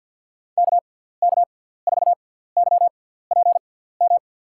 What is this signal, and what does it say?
Morse code, heard as a keyed tone, about 50 wpm, spaced as for 17 wpm, 700 Hz. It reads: KK4YPM
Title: USA Call Signs - 6 Characters - 6x Character Spacing - 50wpm